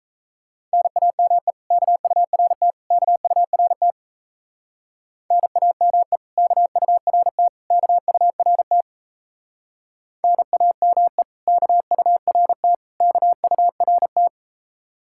Morse code, heard as a keyed tone, about 33 wpm, 700 Hz. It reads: NAME KURT KURT NAME KURT KURT NAME KURT KURT
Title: Speed-Racing Common QSO Phrases Encore - 28wpm